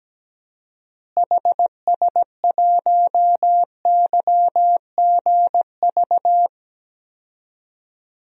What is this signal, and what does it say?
Morse code, heard as a keyed tone, about 17 wpm, 700 Hz. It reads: HS1YGV
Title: DXCC-100 Call Signs - 6 Characters - Repeated 2x - 17wpm